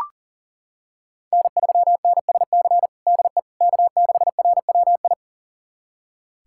Morse code, heard as a keyed tone, about 37 wpm, 700 Hz. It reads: N3NSC DE K6RWI
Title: USA Callsign DE Callsign - 5 Characters - 40wpm